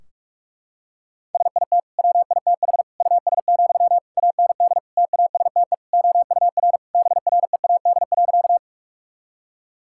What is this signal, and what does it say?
Morse code, heard as a keyed tone, about 41 wpm, 700 Hz. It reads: SIT WITH US, AND TASTE OUR BREAD.
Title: Sentences from Top 1600 Words 45wpm